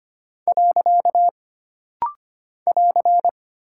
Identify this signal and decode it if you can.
Morse code, heard as a keyed tone, about 25 wpm, 700 Hz. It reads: . <AR>
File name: Single Punctuation-Prosign - Rapid-Fire - 25wpm